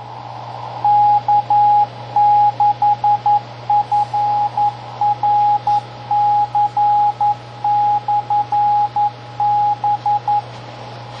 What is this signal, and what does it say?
Morse code, heard as a keyed tone, about 11 wpm, 830 Hz, with radio noise. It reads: K6FRC/B
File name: tracks / K6FRC 920PM